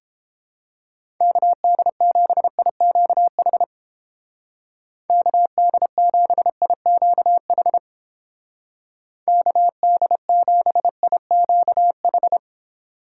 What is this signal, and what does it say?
Morse code, heard as a keyed tone, about 30 wpm, 700 Hz. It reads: KD7IQH KD7IQH KD7IQH